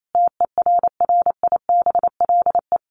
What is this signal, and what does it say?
Morse code, heard as a keyed tone, about 28 wpm, 700 Hz. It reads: TERRIBLE
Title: A Sound Of Thunder 28wpm